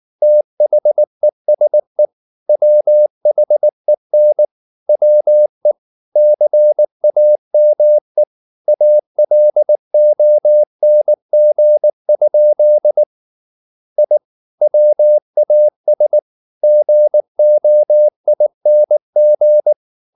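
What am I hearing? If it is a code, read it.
Morse code, heard as a keyed tone, about 19 wpm, 600 Hz. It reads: THESE WHEN WE CAME ALONG? I WAS GOING
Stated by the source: Morse code at 19 wpm from CW med Gnister, reading Ken Follett's novel Never in English.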